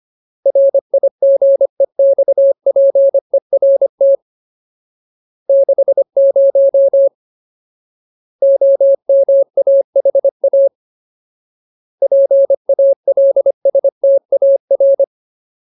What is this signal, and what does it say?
Morse code, heard as a keyed tone, about 25 wpm, 550 Hz. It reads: RIGEXPERT 60 OMAHA PALSTAR
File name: Ultimate Word List / 25WPM Ultimate